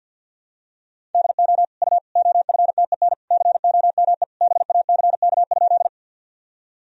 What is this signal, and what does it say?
Morse code, heard as a keyed tone, about 52 wpm, 700 Hz. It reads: DO U OFTEN COME BACK?
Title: Sentences from Top 300 Words and Abbreviations 50wpm